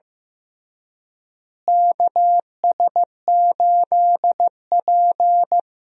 Morse code, heard as a keyed tone, about 15 wpm, 700 Hz. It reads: KS8P
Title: USA Call Signs - 4 Characters - Repeated 3x - 15wpm